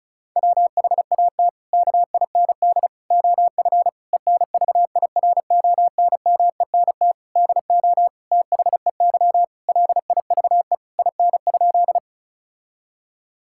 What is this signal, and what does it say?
Morse code, heard as a keyed tone, about 36 wpm, 700 Hz. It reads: WHAT KIND OF ENVIRONMENT DO THEY LIVE IN?
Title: Sentences from Top 1000 Words - Review Part 5 - 35wpm